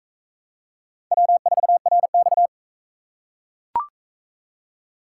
Morse code, heard as a keyed tone, about 37 wpm, 700 Hz. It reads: W4RX